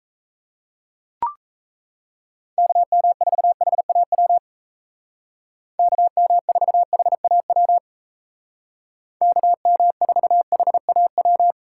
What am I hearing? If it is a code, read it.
Morse code, heard as a keyed tone, about 36 wpm, 700 Hz. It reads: KM4HAW KM4HAW KM4HAW